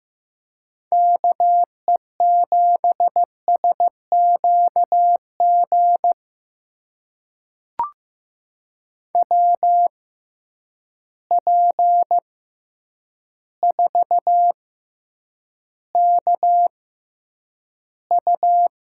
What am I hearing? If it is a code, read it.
Morse code, heard as a keyed tone, about 15 wpm, spaced as for 8 wpm, 700 Hz. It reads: KE7SQG WP4KU
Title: USA Call Signs - 6 Characters - 6x Character Spacing - 15wpm